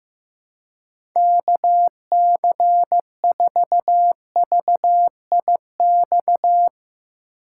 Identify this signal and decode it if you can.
Morse code, heard as a keyed tone, about 15 wpm, 700 Hz. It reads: KC4VIX
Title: USA Call Signs - 6 Characters - 3x Character Spacing - 15wpm